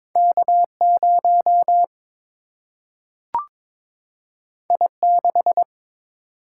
Morse code, heard as a keyed tone, about 22 wpm, 700 Hz. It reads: K 0 I 6
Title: Single-Letter Single-Number 22wpm